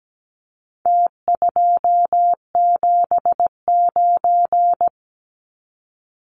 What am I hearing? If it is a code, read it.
Morse code, heard as a keyed tone, about 17 wpm, 700 Hz. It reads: T279
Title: Numbers - Contest Style - 4-Digit 17wpm